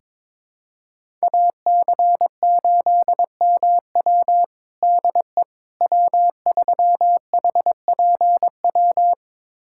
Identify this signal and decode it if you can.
Morse code, heard as a keyed tone, about 22 wpm, 700 Hz. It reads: AC8MW DE W3HPW
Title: USA Callsign DE Callsign - 5 Characters - 22wpm